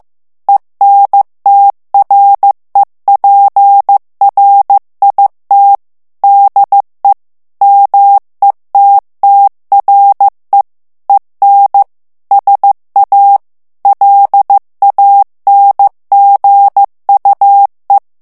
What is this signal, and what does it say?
Morse code, heard as a keyed tone, about 15 wpm, 800 Hz. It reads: ENTREPRIT DE METTRE EN SA LANGUE
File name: morse mp3 / FRANCAIS TEXTE CLAIR 2 VITESSE 14 MOTS MINUTES